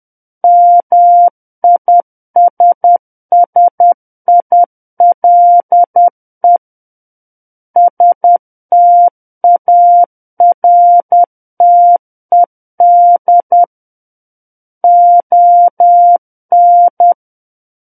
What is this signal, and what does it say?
Morse code, heard as a keyed tone, about 10 wpm, 700 Hz. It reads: MISSILE STARTED ON
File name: Book One / 01-Chapter 1 - 10 WPM